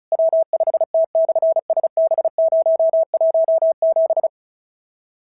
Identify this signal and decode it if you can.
Morse code, heard as a keyed tone, about 36 wpm, 690 Hz. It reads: W5T/SB017
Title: SOTA QSO Practice 35wpm